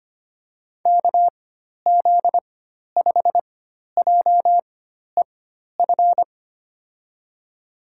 Morse code, heard as a keyed tone, about 25 wpm, spaced as for 12 wpm, 700 Hz. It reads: KZ5JEF
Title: USA Call Signs - 6 Characters - 4x Character Spacing - 25wpm